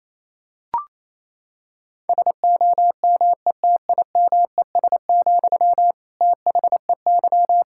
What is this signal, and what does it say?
Morse code, heard as a keyed tone, about 28 wpm, 700 Hz. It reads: SOMETIMES, THEY
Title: Sentences from Top 1000 Words - Review Part 5 - 28wpm